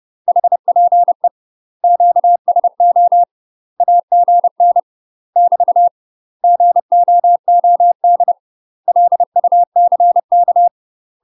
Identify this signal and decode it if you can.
Morse code, heard as a keyed tone, about 30 wpm, 700 Hz. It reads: HPE QSO AGN <BT> GOOD LUCK
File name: UK MASTER QSOs TRAINING / 30WPM / UK MASTER QSOs 0035